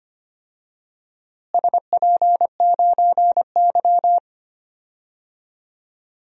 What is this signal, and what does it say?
Morse code, heard as a keyed tone, about 25 wpm, 700 Hz. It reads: SP9Y
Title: DXCC-100 Call Signs - 4 Characters - Repeated 2x - 25wpm